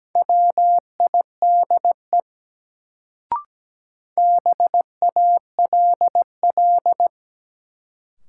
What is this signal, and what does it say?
Morse code, heard as a keyed tone, about 17 wpm, 700 Hz. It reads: WIDE BALL
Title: Sentences from Top 900 Words - New Word Drill - 17wpm